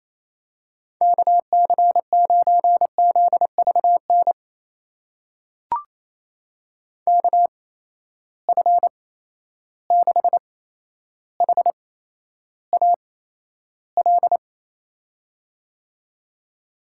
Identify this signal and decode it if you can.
Morse code, heard as a keyed tone, about 28 wpm, spaced as for 12 wpm, 700 Hz. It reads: KC9ZVN KF6HAL
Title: USA Call Signs - 6 Characters - 8x Character Spacing - 28wpm